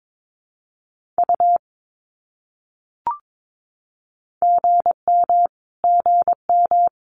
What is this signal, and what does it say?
Morse code, heard as a keyed tone, about 22 wpm, 700 Hz. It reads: U GM GM
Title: Abbreviations Repeated 3x - 22wpm